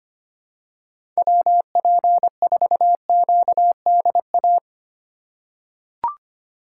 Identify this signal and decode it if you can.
Morse code, heard as a keyed tone, about 25 wpm, 700 Hz. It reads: WP4QDA
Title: USA Call Signs - 3x Character Spacing - 25wpm